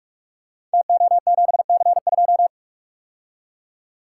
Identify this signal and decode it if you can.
Morse code, heard as a keyed tone, about 44 wpm, 700 Hz. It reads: TO7C2
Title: Code Groups - 5 Letter:Numbers - 45wpm